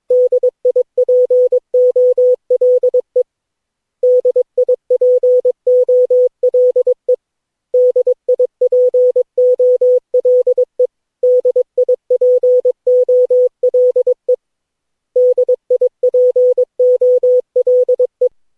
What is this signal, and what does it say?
Morse code, heard as a keyed tone, about 22 wpm, 500 Hz. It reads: DIPOLE DIPOLE DIPOLE DIPOLE DIPOLE
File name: cw / Session9 cw training